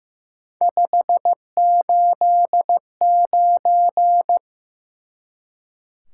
Morse code, heard as a keyed tone, about 15 wpm, 700 Hz. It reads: 589
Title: Top 180 QSO Elements 15wpm